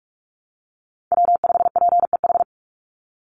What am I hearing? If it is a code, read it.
Morse code, heard as a keyed tone, about 44 wpm, 700 Hz. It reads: W5PEH
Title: Call Signs Repeated 2x - 45wpm